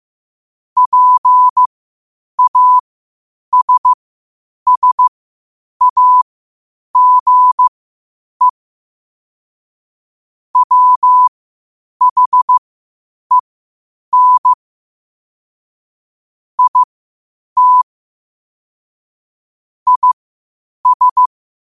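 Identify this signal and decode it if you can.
Morse code, heard as a keyed tone, about 15 wpm, spaced as for 8 wpm, 1000 Hz. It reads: PASSAGE WHEN IT IS
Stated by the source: Morse code quotes at 8 words per minute from QOTD at 8 WPM.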